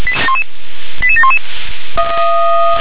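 Five-tone selective call (ZVEI), heard as quick runs of five tones, tones of about 70 ms. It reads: 88711
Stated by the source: ZVEI five-tone call received via SDR.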